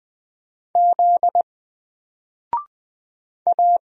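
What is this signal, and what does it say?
Morse code, heard as a keyed tone, about 20 wpm, 700 Hz. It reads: Z A
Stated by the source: Single letter and number Morse drill at 20 wpm, from N4LCD.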